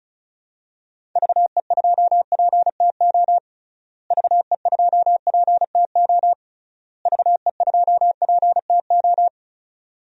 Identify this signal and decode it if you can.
Morse code, heard as a keyed tone, about 36 wpm, 700 Hz. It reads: VE2PTO VE2PTO VE2PTO
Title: DXCC-100 Call Signs - 6 Characters - Repeated 3x - 35wpm